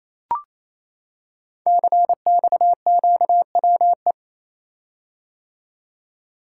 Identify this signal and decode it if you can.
Morse code, heard as a keyed tone, about 27 wpm, 700 Hz. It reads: CXQWE
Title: Code Groups - 5 Letters - 28wpm